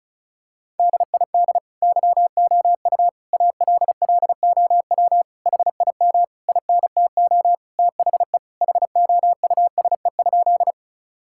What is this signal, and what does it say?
Morse code, heard as a keyed tone, about 34 wpm, 700 Hz. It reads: DID YOU ALLOW HIM INTO THE HOUSE?
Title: Sentences from Top 1000 Words - Review Part 5 - 35wpm